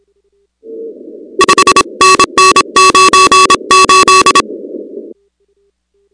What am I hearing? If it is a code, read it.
Morse code, heard as a keyed tone, about 26 wpm, 1130 Hz, with noise. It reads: 5NN 98
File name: contests / 2019-cw-wpx-kt5j-k5pi- -2330 0-38